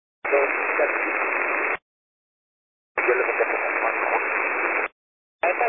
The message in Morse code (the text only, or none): none